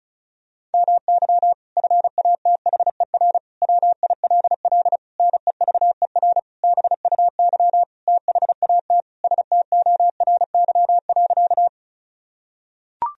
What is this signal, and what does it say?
Morse code, heard as a keyed tone, about 34 wpm, 700 Hz. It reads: MY FATHER WILL NEVER BUY THAT STORY.
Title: Sentences from Top 1000 Words - Review Part 4 - 35wpm